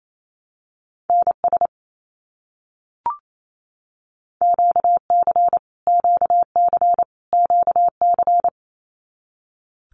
Morse code, heard as a keyed tone, about 28 wpm, 700 Hz. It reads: NS QC QC QC
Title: Canadian Provinces and Territories - Repeated 3x - 28wpm